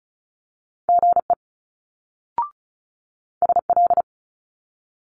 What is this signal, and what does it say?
Morse code, heard as a keyed tone, about 36 wpm, 700 Hz. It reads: GE SL
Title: Multi-letter Phonemes Quick Fire 35wpm